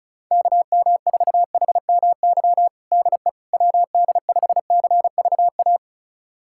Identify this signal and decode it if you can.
Morse code, heard as a keyed tone, about 34 wpm, 700 Hz. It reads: KM4HMY DE WD5CVA
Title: USA Callsign DE Callsign - 6 Characters - 35wpm